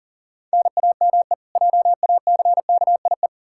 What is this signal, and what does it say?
Morse code, heard as a keyed tone, about 37 wpm, 700 Hz. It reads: NAME JACKIE
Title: Practice QSO - Exchange 1 - Beginning - 40wpm